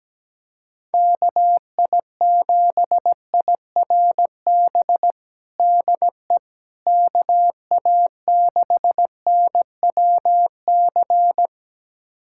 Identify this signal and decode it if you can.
Morse code, heard as a keyed tone, about 17 wpm, 700 Hz. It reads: KI7IRB DE KA6NWC
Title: USA Callsign DE Callsign - 6 Characters - 17wpm